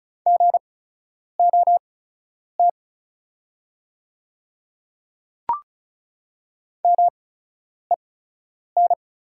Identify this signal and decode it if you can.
Morse code, heard as a keyed tone, about 36 wpm, spaced as for 10 wpm, 700 Hz. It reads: GOT MEN
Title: 8x Character Spacing - Top 300 Words - 35wpm